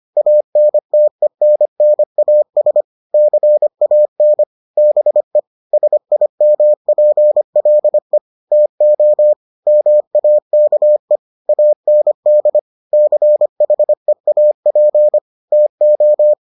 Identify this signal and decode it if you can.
Morse code, heard as a keyed tone, about 25 wpm, 600 Hz. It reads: ANTENNAS CAN BE SIMPLE TO MAKE AND CHEAP TO